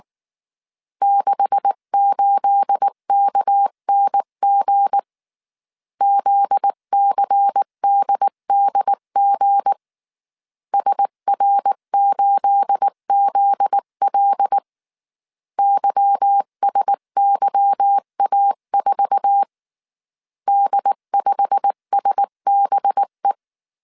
Morse code, heard as a keyed tone, about 19 wpm, 800 Hz. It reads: 68KNG ZCDDG SR8ZL YSYA4 D5SBE
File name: CW / DET / 1 9 1500C S B 18wpm 2551